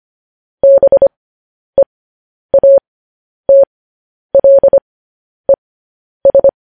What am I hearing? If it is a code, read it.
Morse code, heard as a keyed tone, about 25 wpm, spaced as for 10 wpm, 550 Hz. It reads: BEATLES
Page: news headlines in morse code at 10 wpm